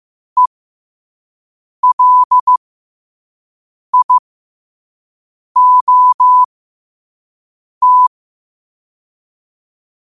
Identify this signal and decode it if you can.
Morse code, heard as a keyed tone, about 15 wpm, spaced as for 5 wpm, 1000 Hz. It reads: ELIOT
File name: qotdfiles / 5wpm